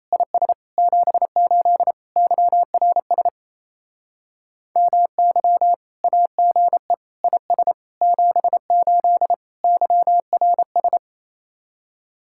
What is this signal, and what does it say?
Morse code, heard as a keyed tone, about 34 wpm, 700 Hz. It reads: IS 78 YRS MY AGE IS 78 YRS
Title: Speed-Racing Common QSO Phrases - 28wpm